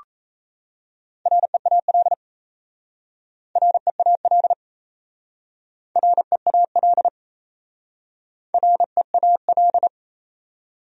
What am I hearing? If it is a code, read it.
Morse code, heard as a keyed tone, about 34 wpm, 700 Hz. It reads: REAL REAL REAL REAL
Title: Speed-Racing Top 500 Words 28wpm